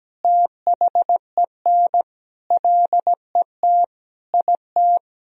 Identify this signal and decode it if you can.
Morse code, heard as a keyed tone, about 17 wpm, 700 Hz. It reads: THEN LET IT
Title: Sentences from Top 1000 Words - Review Part 1 - 17wpm